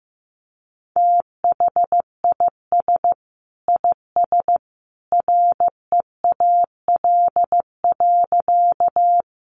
Morse code, heard as a keyed tone, about 15 wpm, 700 Hz. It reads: THIS IS REAL.
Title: Sentences from Top 300 Words 15wpm